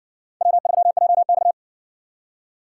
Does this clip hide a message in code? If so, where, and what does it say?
Yes: Morse code from 0.4 to 1.5 s, heard as a keyed tone, about 56 wpm, 700 Hz. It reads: W3JX